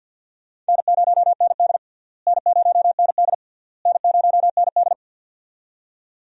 Morse code, heard as a keyed tone, about 53 wpm, 700 Hz. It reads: N0ND N0ND N0ND
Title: USA Call Signs - 4 Characters - Repeated 3x - 50wpm